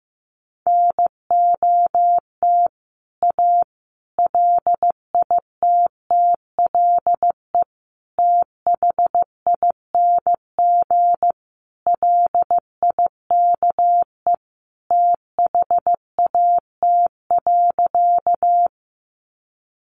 Morse code, heard as a keyed tone, about 15 wpm, 700 Hz. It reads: NOT A LITTLE THING LIKE THAT.
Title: A Sound Of Thunder 15wpm